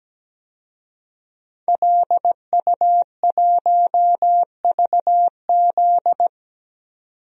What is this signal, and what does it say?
Morse code, heard as a keyed tone, about 17 wpm, 700 Hz. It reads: LU1VZ
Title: DXCC-100 Call Signs - 5 Characters - 17wpm